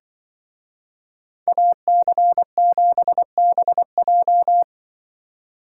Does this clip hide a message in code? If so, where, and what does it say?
Yes: Morse code from 1.5 to 4.6 s, heard as a keyed tone, about 25 wpm, 700 Hz. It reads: AC7BJ